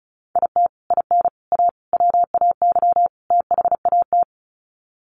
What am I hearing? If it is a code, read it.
Morse code, heard as a keyed tone, about 34 wpm, 700 Hz. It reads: IT IN A WAY THAT
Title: Sets of 5 Words Encore2 35wpm